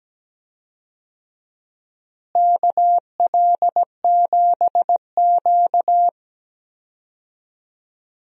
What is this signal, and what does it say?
Morse code, heard as a keyed tone, about 17 wpm, 700 Hz. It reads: KL7Q
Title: DXCC-100 Call Signs - 4 Characters - Repeated 3x - 17wpm